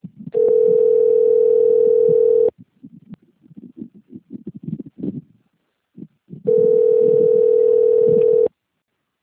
Ringback tone (North America), 440 + 480 Hz.